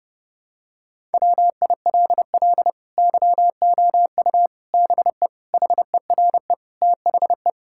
Morse code, heard as a keyed tone, about 30 wpm, 700 Hz. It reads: WILL YOU BE HERE THE
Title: Sentences from Top 1000 Words - Review Part 5 - 30wpm